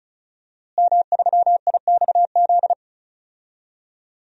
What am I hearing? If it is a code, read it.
Morse code, heard as a keyed tone, about 36 wpm, 700 Hz. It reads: M3IXZ
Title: DXCC-100 Call Signs - 5 Characters - Repeated 3x - 35wpm